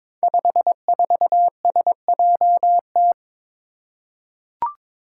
Morse code, heard as a keyed tone, about 23 wpm, 700 Hz. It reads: 54SJT
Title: Five Challenging Letter-Number Combinations 22wpm